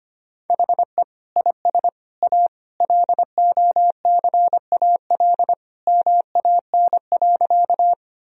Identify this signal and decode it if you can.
Morse code, heard as a keyed tone, about 25 wpm, 700 Hz. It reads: HE IS A LOCAL MAN.
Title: Sentences from Top 1000 Words - Review Part 3 - 25wpm